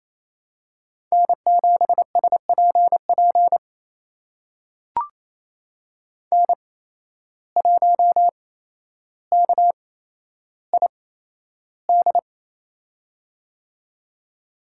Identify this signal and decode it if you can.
Morse code, heard as a keyed tone, about 28 wpm, spaced as for 12 wpm, 700 Hz. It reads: N7SPP N1KID
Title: USA Call Signs - 5 Characters - 8x Character Spacing - 28wpm